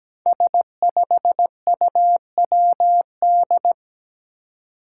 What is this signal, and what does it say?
Morse code, heard as a keyed tone, about 17 wpm, 700 Hz. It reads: S5UWD
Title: Five Challenging Letter-Number Combinations 17wpm